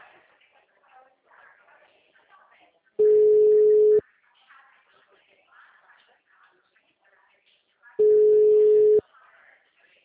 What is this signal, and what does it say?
Ringback tone (Europe, 425 Hz), with speech.
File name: out- -2011- - - .6